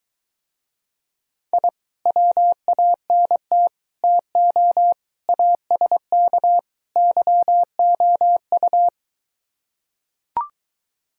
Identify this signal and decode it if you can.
Morse code, heard as a keyed tone, about 23 wpm, 700 Hz. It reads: I WANT TO ASK YOU